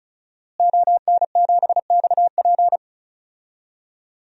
Morse code, heard as a keyed tone, about 34 wpm, 700 Hz. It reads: ON7XP
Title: DXCC-100 Call Signs - 5 Characters - Repeated 2x - 35wpm